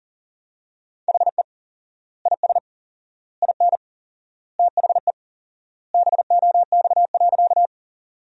Morse code, heard as a keyed tone, about 40 wpm, 700 Hz. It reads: HE IS IN THE BOX.
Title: 4x Word-Spacing - Sentences from Top 500 Words - 40wpm